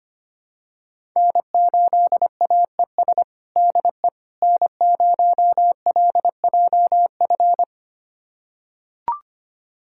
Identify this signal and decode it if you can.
Morse code, heard as a keyed tone, about 25 wpm, 700 Hz. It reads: N8AES DE N0LJF
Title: USA Callsign DE Callsign - 5 Characters - 25wpm